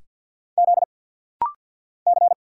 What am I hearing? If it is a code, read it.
Morse code, heard as a keyed tone, about 50 wpm, 690 Hz. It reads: Z C
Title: Single Letter-Number Alt - Rapid-Fire 50wpm